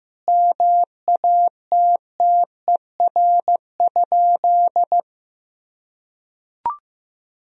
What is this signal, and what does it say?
Morse code, heard as a keyed tone, about 16 wpm, 700 Hz. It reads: MATTER?
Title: Sentences from Top 1000 Words - Review Part 4 - 15wpm